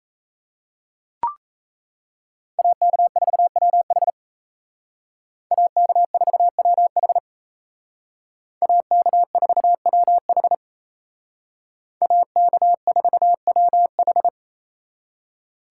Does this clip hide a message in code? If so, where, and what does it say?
Yes: Morse code from 2.6 to 14.3 s, heard as a keyed tone, about 36 wpm, 700 Hz. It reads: AK4WH AK4WH AK4WH AK4WH